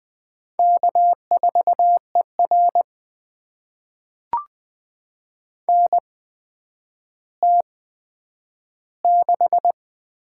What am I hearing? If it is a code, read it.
Morse code, heard as a keyed tone, about 20 wpm, spaced as for 10 wpm, 700 Hz. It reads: K4ER NT6
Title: USA Call Signs - 4 Characters - 8x Character Spacing - 20wpm